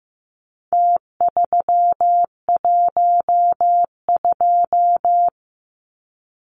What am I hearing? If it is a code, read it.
Morse code, heard as a keyed tone, about 15 wpm, 700 Hz. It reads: T312
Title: Numbers - Contest Style - 4-Digit 15wpm